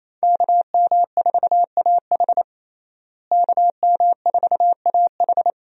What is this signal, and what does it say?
Morse code, heard as a keyed tone, about 28 wpm, 700 Hz. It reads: KM4AH KM4AH
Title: USA Call Signs - 5 Characters - Repeated 2x - 28wpm